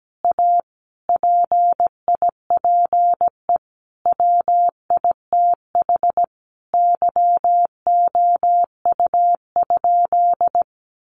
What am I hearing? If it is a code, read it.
Morse code, heard as a keyed tone, about 17 wpm, 700 Hz. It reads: A PIPE WITH YOU?
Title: Sentences from Top 1500 Words 17wpm